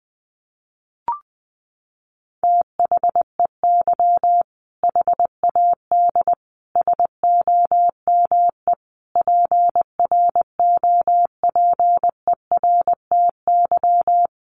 Morse code, heard as a keyed tone, about 20 wpm, 700 Hz. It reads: THEY HAD SOME PROPERTY
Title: Sentences from Top 1500 Words 20wpm